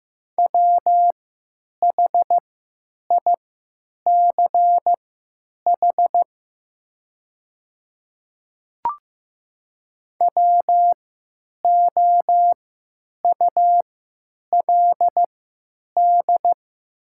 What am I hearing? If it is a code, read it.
Morse code, heard as a keyed tone, about 15 wpm, spaced as for 8 wpm, 700 Hz. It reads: WHICH WOULD
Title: 3x Character Spacing - Top 100 Words - 15wpm